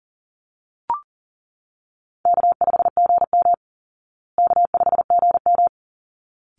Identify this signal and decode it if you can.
Morse code, heard as a keyed tone, about 40 wpm, 700 Hz. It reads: K5GM K5GM
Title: Call Signs Repeated 3x - 40wpm